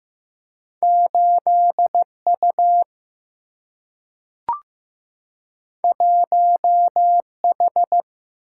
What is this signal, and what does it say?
Morse code, heard as a keyed tone, about 15 wpm, 700 Hz. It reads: 8 U 1 H
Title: Single-Letter Single-Number 15wpm